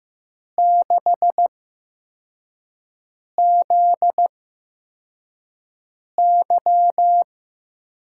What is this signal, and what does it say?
Morse code, heard as a keyed tone, about 15 wpm, spaced as for 4 wpm, 700 Hz. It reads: 6ZY
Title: USA Call Signs - 5 Characters - 8x Character Spacing - 15wpm